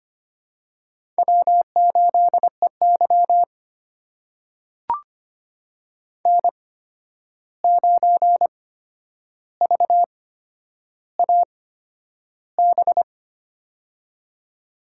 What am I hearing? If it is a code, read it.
Morse code, heard as a keyed tone, about 25 wpm, spaced as for 10 wpm, 700 Hz. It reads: W8EY N9VAB
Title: USA Call Signs - 8x Character Spacing - 25wpm